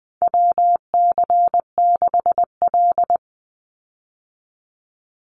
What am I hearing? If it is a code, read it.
Morse code, heard as a keyed tone, about 20 wpm, 700 Hz. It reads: WC6L
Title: USA Call Signs 20wpm